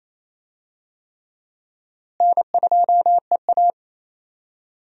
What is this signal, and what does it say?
Morse code, heard as a keyed tone, about 28 wpm, 700 Hz. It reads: N2EA
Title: USA Call Signs - 4 Characters - Repeated 3x - 28wpm